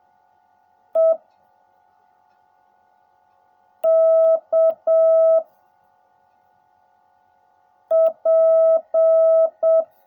Morse code, heard as a keyed tone, about 7 wpm, 660 Hz. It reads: E K P
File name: jh8cbh / 2021cw / cw16a